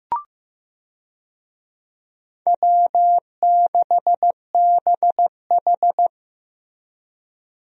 Morse code, heard as a keyed tone, about 15 wpm, 700 Hz. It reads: W6BH
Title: USA Call Signs - 4 Characters - Repeated 2x - 15wpm